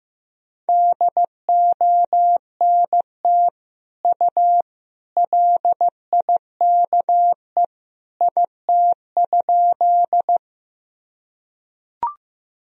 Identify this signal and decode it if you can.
Morse code, heard as a keyed tone, about 15 wpm, 700 Hz. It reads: DONT U LIKE IT?
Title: Sentences from Top 300 Words and Abbreviations 15wpm